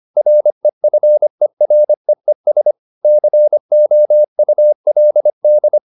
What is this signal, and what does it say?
Morse code, heard as a keyed tone, about 25 wpm, 600 Hz. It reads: REFEREES COULD